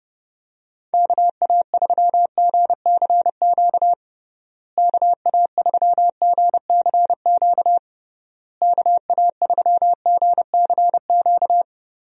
Morse code, heard as a keyed tone, about 31 wpm, 700 Hz. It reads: KA3GCQ KA3GCQ KA3GCQ
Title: USA Call Signs - 6 Characters - Repeated 3x - 30wpm